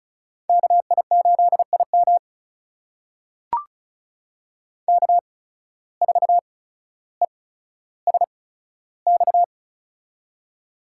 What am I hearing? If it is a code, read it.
Morse code, heard as a keyed tone, about 36 wpm, spaced as for 15 wpm, 700 Hz. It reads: KI8IM K4ESX
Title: USA Call Signs - 5 Characters - 8x Character Spacing - 35wpm